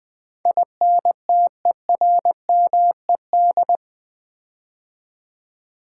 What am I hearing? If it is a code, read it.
Morse code, heard as a keyed tone, about 20 wpm, 700 Hz. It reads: INTERMED
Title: Build A Word - 4 Letters At A Time - 12-Letter Words - 20wpm